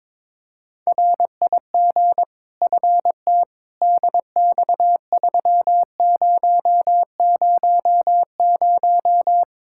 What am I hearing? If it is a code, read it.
Morse code, heard as a keyed tone, about 22 wpm, 700 Hz. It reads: RIG FT DX3000
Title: K1USN SST QSO Practice 22wpm